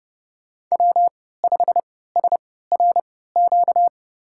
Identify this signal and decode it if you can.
Morse code, heard as a keyed tone, about 30 wpm, spaced as for 17 wpm, 700 Hz. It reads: W5SRQ
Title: USA Call Signs - 5 Characters - 3x Character Spacing - 30wpm